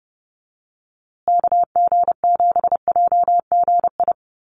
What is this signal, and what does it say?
Morse code, heard as a keyed tone, about 30 wpm, 700 Hz. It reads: KG7JGI
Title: USA Call Signs Repeated 3x 30wpm